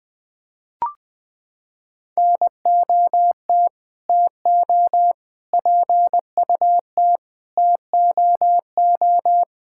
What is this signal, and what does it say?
Morse code, heard as a keyed tone, about 20 wpm, 700 Hz. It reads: NOT TO PUT TOO
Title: Sets of 9 Words - 20wpm